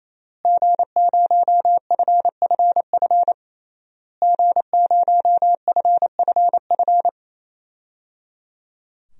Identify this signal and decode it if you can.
Morse code, heard as a keyed tone, about 28 wpm, 700 Hz. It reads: G0FFF G0FFF
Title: DXCC-100 Call Signs - 5 Characters - Repeated 3x - 28wpm